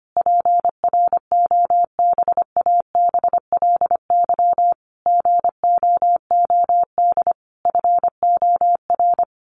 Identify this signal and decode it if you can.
Morse code, heard as a keyed tone, about 25 wpm, 700 Hz. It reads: PROBABLY GOOD FOR
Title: Sentences from Top 1400 Words 25wpm